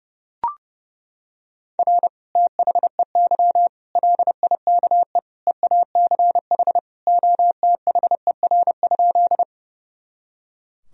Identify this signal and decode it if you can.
Morse code, heard as a keyed tone, about 32 wpm, 700 Hz. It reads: R THEY LIKE EACH OTHER?
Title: Sentences from Top 200 Words and Abbreviations 30wpm